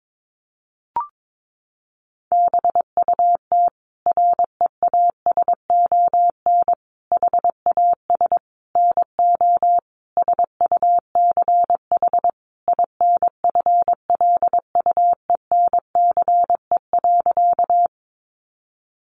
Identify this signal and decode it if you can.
Morse code, heard as a keyed tone, about 22 wpm, 700 Hz. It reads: BUT REASON HAS NO SUCH INFLUENCE.
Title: Sentences from Top 1100 Words 22wpm